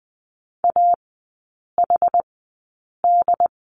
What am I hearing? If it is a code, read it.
Morse code, heard as a keyed tone, about 20 wpm, spaced as for 8 wpm, 700 Hz. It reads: AHD
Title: 178 Lesson 29e - Slash - Three Characters 20wpm